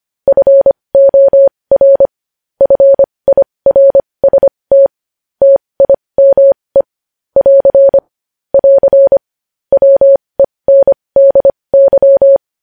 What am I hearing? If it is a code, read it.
Morse code, heard as a keyed tone, about 25 wpm, spaced as for 20 wpm, 550 Hz. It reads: FOR FIRST TIME <AR> <AR> WENDY